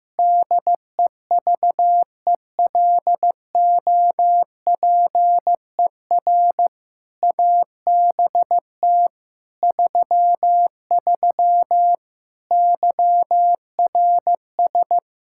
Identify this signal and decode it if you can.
Morse code, heard as a keyed tone, about 15 wpm, 700 Hz. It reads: DEVELOPER ABT 33 YRS
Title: K1USN SST QSO Practice 15wpm